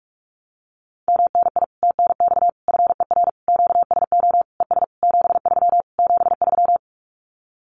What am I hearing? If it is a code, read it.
Morse code, heard as a keyed tone, about 44 wpm, 700 Hz. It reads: MNI TNX FER QSO ES 73 73
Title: Common QSO Phrases Encore 45wpm